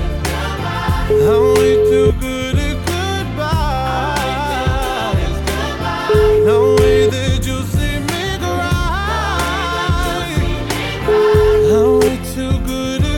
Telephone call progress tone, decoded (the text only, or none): ringback tone